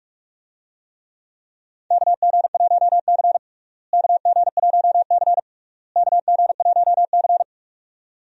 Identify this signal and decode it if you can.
Morse code, heard as a keyed tone, about 44 wpm, 700 Hz. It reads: KG1C KG1C KG1C
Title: USA Call Signs - 4 Characters - Repeated 3x - 45wpm